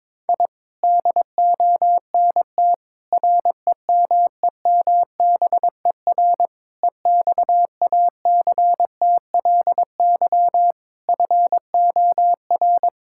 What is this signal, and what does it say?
Morse code, heard as a keyed tone, about 22 wpm, 700 Hz. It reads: I DONT REMEMBER EXACTLY FOR
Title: Sentences from Top 1000 Words - Review Part 6 - 22wpm